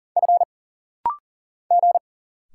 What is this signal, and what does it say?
Morse code, heard as a keyed tone, about 39 wpm, 700 Hz. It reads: F G
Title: Single Letters - Rapid-Fire 40wpm